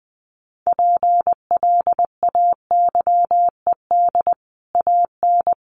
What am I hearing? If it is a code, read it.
Morse code, heard as a keyed tone, about 20 wpm, 700 Hz. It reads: PLAYED AN
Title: Sets of 5 Words Encore2 20wpm